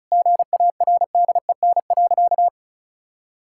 Morse code, heard as a keyed tone, about 36 wpm, 700 Hz. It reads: GARDEN.
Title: Sentences from Top 1000 Words - Review Part 5 - 35wpm